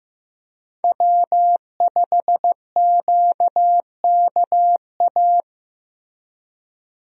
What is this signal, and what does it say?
Morse code, heard as a keyed tone, about 15 wpm, 700 Hz. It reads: W5QKA
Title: USA Call Signs - 5 Characters - Repeated 2x - 15wpm